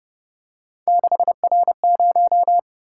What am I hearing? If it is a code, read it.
Morse code, heard as a keyed tone, about 31 wpm, 700 Hz. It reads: 6R0